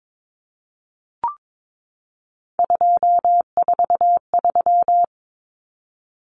Morse code, heard as a keyed tone, about 22 wpm, 700 Hz. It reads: 243
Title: Common Cut Numbers - 3-Digit 22wpm